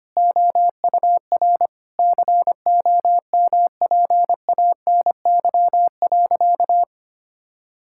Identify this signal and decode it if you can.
Morse code, heard as a keyed tone, about 25 wpm, 700 Hz. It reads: OUR COMPANY.
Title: Sentences from Top 1000 Words - Review Part 5 - 25wpm